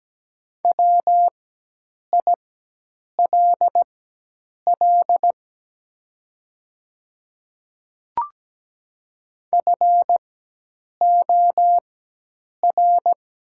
Morse code, heard as a keyed tone, about 17 wpm, spaced as for 8 wpm, 700 Hz. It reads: WILL FOR
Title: 4x Character Spacing - Top 200 Words - 17wpm